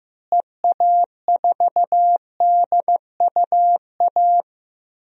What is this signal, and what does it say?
Morse code, heard as a keyed tone, about 15 wpm, 700 Hz. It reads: EA4DUA
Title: DXCC-100 Call Signs - 6 Characters - Repeated 3x - 15wpm